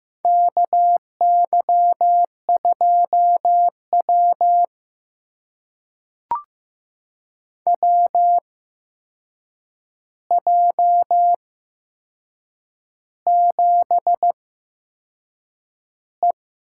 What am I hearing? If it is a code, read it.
Morse code, heard as a keyed tone, about 15 wpm, spaced as for 6 wpm, 700 Hz. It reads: KY2W WJ7E
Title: USA Call Signs - 4 Characters - 8x Character Spacing - 15wpm